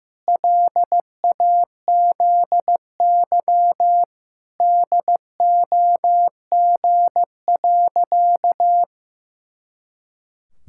Morse code, heard as a keyed tone, about 15 wpm, 700 Hz. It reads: LAZY DOG.
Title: Pangram Challenge v2 - 15wpm